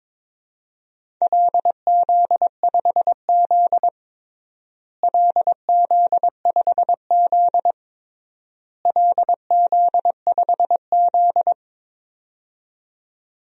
Morse code, heard as a keyed tone, about 21 wpm, 700 Hz. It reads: LZ5Z LZ5Z LZ5Z
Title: DXCC-100 Call Signs - 4 Characters - Repeated 3x - 22wpm